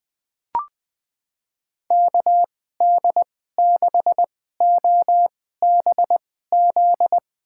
Morse code, heard as a keyed tone, about 20 wpm, spaced as for 14 wpm, 700 Hz. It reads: KD6OBZ
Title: USA Call Signs - 6 Characters - 2x Character Spacing - 20wpm